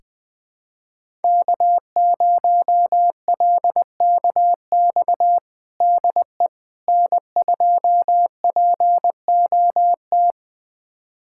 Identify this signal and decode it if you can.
Morse code, heard as a keyed tone, about 21 wpm, 700 Hz. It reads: K0LKX DE N2POT
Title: USA Callsign DE Callsign - 5 Characters - 20wpm